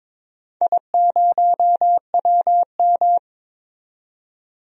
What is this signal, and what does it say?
Morse code, heard as a keyed tone, about 22 wpm, 700 Hz. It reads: I0WM
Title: DXCC-100 Call Signs - 4 Characters - 22wpm